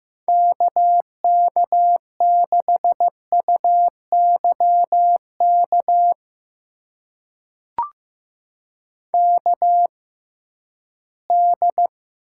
Morse code, heard as a keyed tone, about 15 wpm, spaced as for 11 wpm, 700 Hz. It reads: KK6UYK KD
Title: USA Call Signs - 6 Characters - 6x Character Spacing - 15wpm